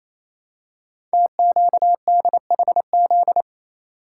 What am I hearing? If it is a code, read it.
Morse code, heard as a keyed tone, about 28 wpm, 700 Hz. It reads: TQDHZ
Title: Code Groups - 5 Letter:Numbers - 28wpm